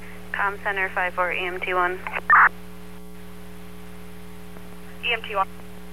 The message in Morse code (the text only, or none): none